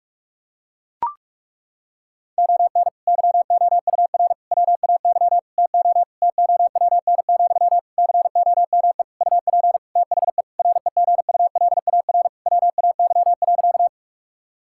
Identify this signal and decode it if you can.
Morse code, heard as a keyed tone, about 42 wpm, 700 Hz. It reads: ON YOUR WAY TO TOWN, COME UP THE REGULAR WAY.
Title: Sentences from Top 1000 Words - Review Part 6 - 45wpm